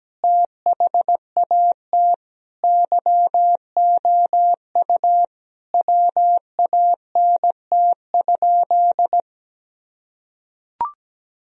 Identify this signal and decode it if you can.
Morse code, heard as a keyed tone, about 17 wpm, 700 Hz. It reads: THAT YOU WANT?
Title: Sentences from Top 1000 Words - Review Part 4 - 17wpm